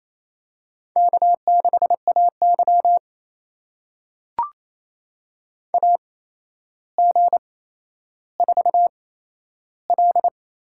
Morse code, heard as a keyed tone, about 28 wpm, spaced as for 12 wpm, 700 Hz. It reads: K6AY AG4L
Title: USA Call Signs - 8x Character Spacing - 28wpm